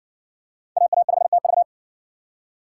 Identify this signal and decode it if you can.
Morse code, heard as a keyed tone, about 60 wpm, 700 Hz. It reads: AA5TV